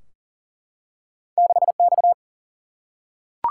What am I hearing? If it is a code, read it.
Morse code, heard as a keyed tone, about 38 wpm, 700 Hz. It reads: 6 X
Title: Single-Letter Single-Number 40wpm